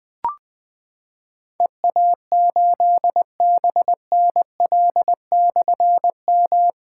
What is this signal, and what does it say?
Morse code, heard as a keyed tone, about 20 wpm, 700 Hz. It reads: EA8BNL/M
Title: Difficult Call Signs 20wpm